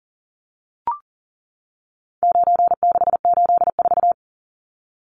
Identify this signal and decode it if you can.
Morse code, heard as a keyed tone, about 40 wpm, 700 Hz. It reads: 9684
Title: Numbers - 4-Digit 40wpm